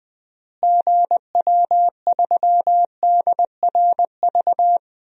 Morse code, heard as a keyed tone, about 21 wpm, 700 Hz. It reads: GW3DRV
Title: DXCC-100 Call Signs - 6 Characters - Repeated 3x - 20wpm